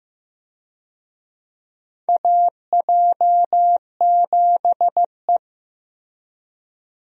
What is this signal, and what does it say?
Morse code, heard as a keyed tone, about 15 wpm, 700 Hz. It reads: AJ7E
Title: USA Call Signs - 4 Characters - Repeated 2x - 15wpm